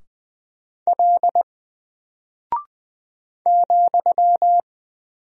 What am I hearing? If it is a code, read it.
Morse code, heard as a keyed tone, about 20 wpm, 700 Hz. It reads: L ,
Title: Single Character (Letter:Number:Punctuation:Prosign) 20wpm